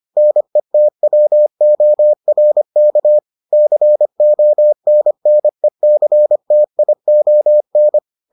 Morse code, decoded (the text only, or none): NETWORK CONNECTION